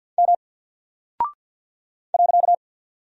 Morse code, heard as a keyed tone, about 48 wpm, 700 Hz. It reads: M .
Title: Single Character (Letter:Number:Punctuation:Prosign) Alt 50wpm